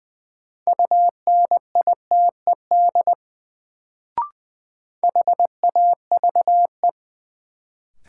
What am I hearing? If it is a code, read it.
Morse code, heard as a keyed tone, about 21 wpm, 700 Hz. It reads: UNITED HAVE
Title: Sets of 2 Words - New Word Drill - 20wpm